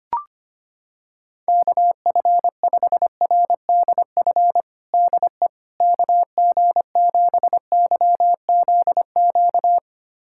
Morse code, heard as a keyed tone, about 25 wpm, 700 Hz. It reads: KF5RDF DE KG7YZQ
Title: USA Callsign DE Callsign - 6 Characters - 25wpm